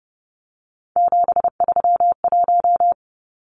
Morse code, heard as a keyed tone, about 30 wpm, 700 Hz. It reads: 731
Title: Numbers - 3-Digit 30wpm